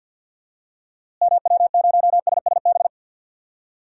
Morse code, heard as a keyed tone, about 47 wpm, 700 Hz. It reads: MW0SSB
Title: DXCC-100 Call Signs - 6 Characters - Repeated 3x - 50wpm